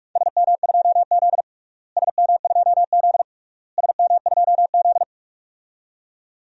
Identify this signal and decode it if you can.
Morse code, heard as a keyed tone, about 46 wpm, 690 Hz. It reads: SM2Z SM2Z SM2Z
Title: DXCC-100 Call Signs - 4 Characters - Repeated 3x - 45wpm